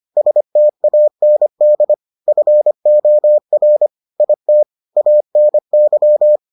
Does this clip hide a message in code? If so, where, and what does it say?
Yes: Morse code from 0.2 to 6.4 s, heard as a keyed tone, about 25 wpm, 600 Hz. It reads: STAND FOR IT ANY